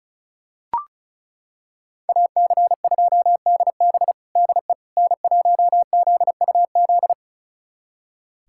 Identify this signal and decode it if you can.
Morse code, heard as a keyed tone, about 36 wpm, 700 Hz. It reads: AC2DB DE N1ZUZ
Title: USA Callsign DE Callsign - 5 Characters - 35wpm